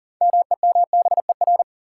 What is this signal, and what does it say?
Morse code, heard as a keyed tone, about 40 wpm, 700 Hz. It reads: MEMBER
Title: Sentences from Top 1100 Words - New Word Drill - 40wpm